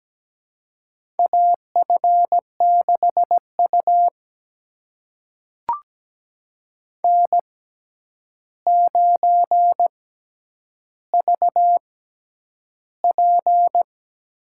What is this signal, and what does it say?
Morse code, heard as a keyed tone, about 17 wpm, spaced as for 9 wpm, 700 Hz. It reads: AF6U N9VP
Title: USA Call Signs - 4 Characters - 6x Character Spacing - 17wpm